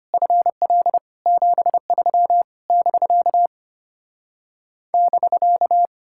Morse code, heard as a keyed tone, about 33 wpm, 700 Hz. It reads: FL 73 <BK> <BK>